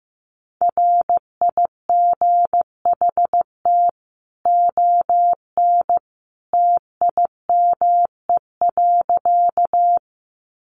Morse code, heard as a keyed tone, about 14 wpm, 700 Hz. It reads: RIGHT ON TIME.
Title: A Sound Of Thunder 15wpm